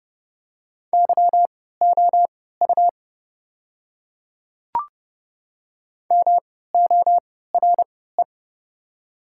Morse code, heard as a keyed tone, about 29 wpm, spaced as for 17 wpm, 690 Hz. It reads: YOU MORE
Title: 3x Character Spacing - Top 200 Words - 30wpm